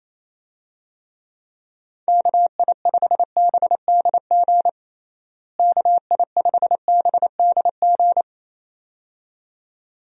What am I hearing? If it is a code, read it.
Morse code, heard as a keyed tone, about 28 wpm, 700 Hz. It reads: KI5BDG KI5BDG
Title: USA Call Signs - 6 Characters - Repeated 2x - 28wpm